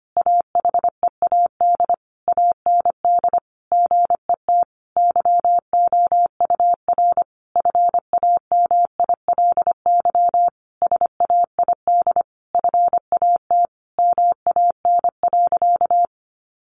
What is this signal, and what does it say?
Morse code, heard as a keyed tone, about 25 wpm, 700 Hz. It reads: AHEAD AND GET YOUR FAMILY SAID FAT MAN.
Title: Sentences from Top 1100 Words 25wpm